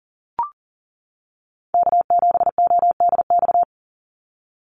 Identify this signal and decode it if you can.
Morse code, heard as a keyed tone, about 40 wpm, 700 Hz. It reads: K7ODX
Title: NA SOTA Chaser Callsigns 40wpm